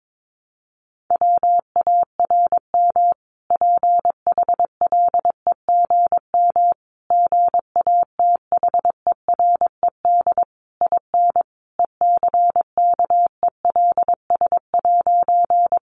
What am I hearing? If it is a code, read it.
Morse code, heard as a keyed tone, about 22 wpm, 700 Hz. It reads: WARM PHLEGM GATHERED IN ECKELS'
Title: A Sound Of Thunder 22wpm